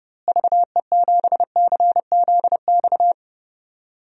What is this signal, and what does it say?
Morse code, heard as a keyed tone, about 30 wpm, 700 Hz. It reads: VE7CZX
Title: DXCC-100 Call Signs - 6 Characters - Repeated 2x - 30wpm